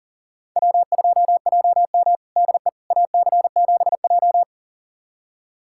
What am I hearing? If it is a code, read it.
Morse code, heard as a keyed tone, about 40 wpm, 700 Hz. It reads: W2JM DE AC7J
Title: USA Callsign DE Callsign 40wpm